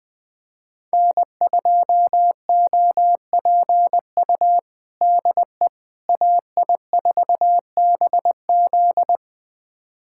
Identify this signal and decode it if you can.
Morse code, heard as a keyed tone, about 20 wpm, 700 Hz. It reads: N2OPU DE AI4BZ
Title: USA Callsign DE Callsign 20wpm